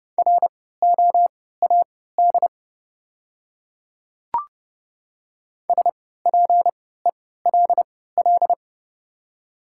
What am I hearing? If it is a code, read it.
Morse code, heard as a keyed tone, about 31 wpm, spaced as for 17 wpm, 700 Hz. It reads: ROAD SPELL
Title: 3x Character Spacing - Top 500 Words - 30wpm